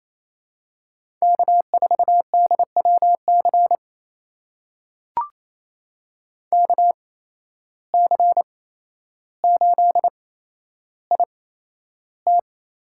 Morse code, heard as a keyed tone, about 28 wpm, spaced as for 12 wpm, 700 Hz. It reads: K4DWC KC8IT
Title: USA Call Signs - 8x Character Spacing - 28wpm